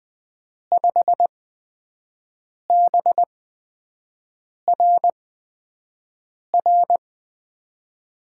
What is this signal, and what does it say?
Morse code, heard as a keyed tone, about 21 wpm, spaced as for 5 wpm, 700 Hz. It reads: 5BRR
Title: USA Call Signs - 8x Character Spacing - 20wpm